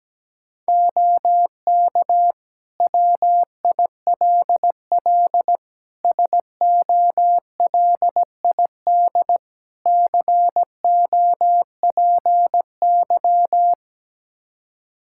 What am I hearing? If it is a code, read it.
Morse code, heard as a keyed tone, about 17 wpm, 700 Hz. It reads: OK WILL SOLID COPY
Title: Practice QSO - Exchange 3 - More Info - 17wpm